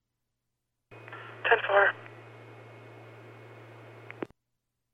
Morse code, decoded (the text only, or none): none